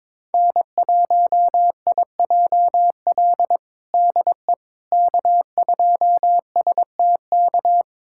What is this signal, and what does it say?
Morse code, heard as a keyed tone, about 22 wpm, 700 Hz. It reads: N1IJL DE K2STK
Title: USA Callsign DE Callsign - 5 Characters - 22wpm